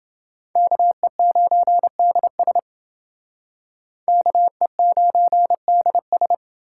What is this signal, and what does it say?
Morse code, heard as a keyed tone, about 28 wpm, 700 Hz. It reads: KE9DS KE9DS